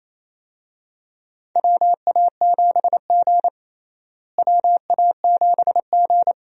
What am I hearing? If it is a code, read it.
Morse code, heard as a keyed tone, about 28 wpm, 700 Hz. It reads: WA7G WA7G
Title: USA Call Signs - 4 Characters - Repeated 2x - 28wpm